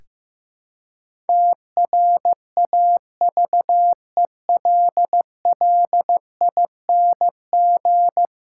Morse code, heard as a keyed tone, about 15 wpm, 700 Hz. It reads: TRAVELLING
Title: Pangram Challenge v2 - 15wpm